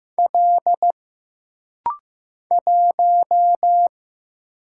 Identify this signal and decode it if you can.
Morse code, heard as a keyed tone, about 15 wpm, 700 Hz. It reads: L 1
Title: Single Letter-Number Alt - Rapid-Fire 15wpm